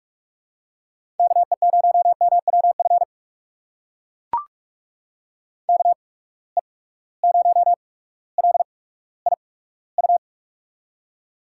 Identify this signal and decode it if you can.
Morse code, heard as a keyed tone, about 44 wpm, spaced as for 19 wpm, 700 Hz. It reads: KE0MWF KE0LIU
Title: USA Call Signs - 6 Characters - 8x Character Spacing - 45wpm